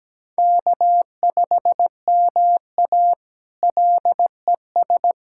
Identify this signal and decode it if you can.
Morse code, heard as a keyed tone, about 17 wpm, 700 Hz. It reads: K5MA LES
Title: K1USN SST QSO Practice - 17wpm